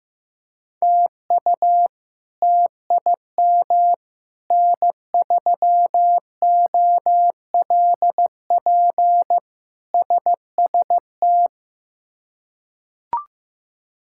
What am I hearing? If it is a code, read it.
Morse code, heard as a keyed tone, about 15 wpm, 700 Hz. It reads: TU TIM N3OLP SST
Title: K1USN SST QSO Practice - 15wpm